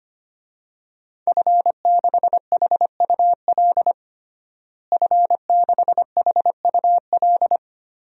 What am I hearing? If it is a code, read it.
Morse code, heard as a keyed tone, about 25 wpm, 700 Hz. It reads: F6HUL F6HUL
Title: DXCC-100 Call Signs - 5 Characters - Repeated 2x - 25wpm